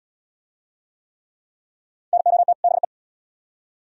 Morse code, heard as a keyed tone, about 77 wpm, 700 Hz. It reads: NOT BE